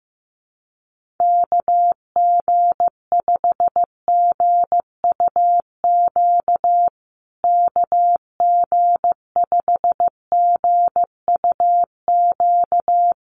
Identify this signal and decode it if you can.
Morse code, heard as a keyed tone, about 15 wpm, 700 Hz. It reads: KG5GUQ KG5GUQ
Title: USA Call Signs Repeated 3x 15wpm